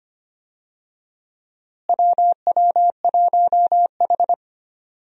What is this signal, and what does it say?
Morse code, heard as a keyed tone, about 25 wpm, 700 Hz. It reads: WW1H
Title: USA Call Signs - 4 Characters - Repeated 3x - 25wpm